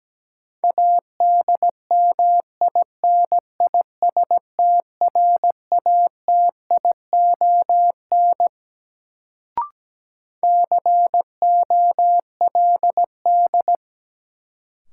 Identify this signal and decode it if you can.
Morse code, heard as a keyed tone, about 17 wpm, 700 Hz. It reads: ADMINISTRATION COLD
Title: Sets of 5 Words - New Word Drill - 17wpm